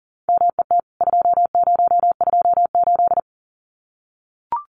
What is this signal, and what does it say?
Morse code, heard as a keyed tone, about 40 wpm, 700 Hz. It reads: MET 2028
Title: CWT Exchange 40wpm